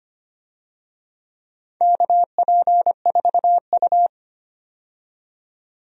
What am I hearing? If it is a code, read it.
Morse code, heard as a keyed tone, about 25 wpm, 700 Hz. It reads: KP4U
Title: DXCC-100 Call Signs - 4 Characters - 25wpm